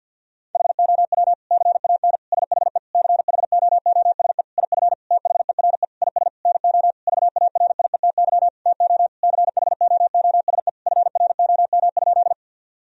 Morse code, heard as a keyed tone, about 50 wpm, 700 Hz. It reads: HOW CAN SHE CHOOSE IF THERE IS NO VARIETY TO CHOOSE FROM?
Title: Sentences from Top 1000 Words - Review Part 4 - 50wpm